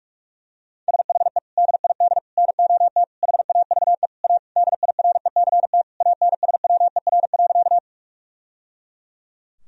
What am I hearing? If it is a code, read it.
Morse code, heard as a keyed tone, about 42 wpm, 700 Hz. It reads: SHE DID NOT HAVE A DIRECT ANSWER.
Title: Sentences from Top 1000 Words - Review Part 2 - 45wpm